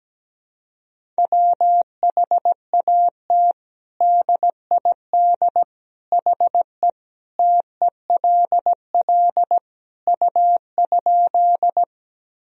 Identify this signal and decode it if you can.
Morse code, heard as a keyed tone, about 17 wpm, 700 Hz. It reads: WHAT DID HE TELL U?
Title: Sentences from Top 200 Words and Abbreviations 17wpm